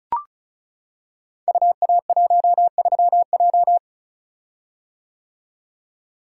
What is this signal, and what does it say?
Morse code, heard as a keyed tone, about 36 wpm, 700 Hz. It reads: UA13J
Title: Code Groups - 5 Letter:Numbers - 35wpm